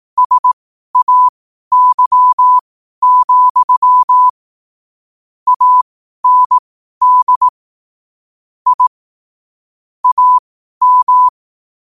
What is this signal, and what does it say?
Morse code, heard as a keyed tone, about 18 wpm, spaced as for 12 wpm, 1000 Hz. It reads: SAY, AND I AM
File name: qotdfiles / 12wpm